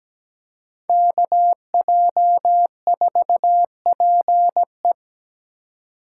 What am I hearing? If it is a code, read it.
Morse code, heard as a keyed tone, about 17 wpm, 700 Hz. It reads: KJ4PE
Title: USA Call Signs - 5 Characters - 3x Character Spacing - 17wpm